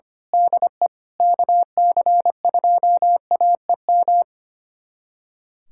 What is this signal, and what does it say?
Morse code, heard as a keyed tone, about 25 wpm, 700 Hz. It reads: DE KC2AEM
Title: USA Callsign DE Callsign 25wpm